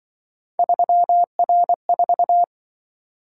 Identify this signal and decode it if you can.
Morse code, heard as a keyed tone, about 25 wpm, 700 Hz. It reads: 3R4
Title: Speed-Racing - Three Letter-Number Combinations - 20wpm